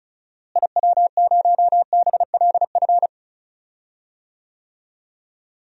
Morse code, heard as a keyed tone, about 36 wpm, 700 Hz. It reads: IW0BLF
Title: DXCC-100 Call Signs - 6 Characters - Repeated 3x - 35wpm